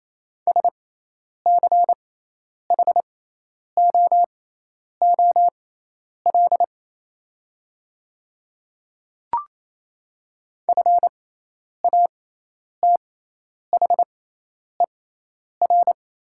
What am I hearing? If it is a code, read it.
Morse code, heard as a keyed tone, about 28 wpm, spaced as for 10 wpm, 700 Hz. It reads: SCHOOL FATHER
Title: 6x Character Spacing - Top 300 Words - 28wpm